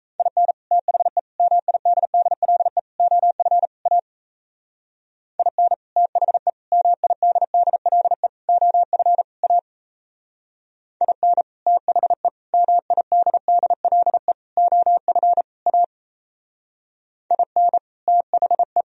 Morse code, heard as a keyed tone, about 33 wpm, 700 Hz. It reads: IN THE MIDDLE OF A IN THE MIDDLE OF A IN THE MIDDLE OF A IN THE
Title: Speed-Racing - Sets of 5 Words 28wpm